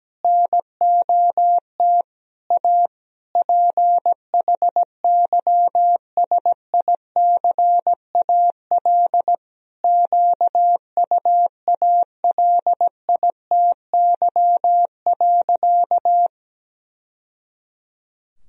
Morse code, heard as a keyed tone, about 17 wpm, 700 Hz. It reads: NOT A PHYSICAL QUALITY.
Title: Sentences from Top 1000 Words - Review Part 4 - 17wpm